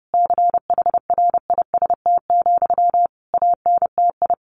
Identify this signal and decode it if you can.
Morse code, heard as a keyed tone, about 30 wpm, 700 Hz. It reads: CHRIST, ANTI
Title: A Sound Of Thunder 30wpm